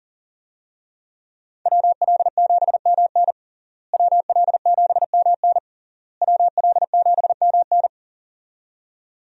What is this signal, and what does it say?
Morse code, heard as a keyed tone, about 40 wpm, 700 Hz. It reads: WL7MN WL7MN WL7MN
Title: USA Call Signs - 5 Characters - Repeated 3x - 40wpm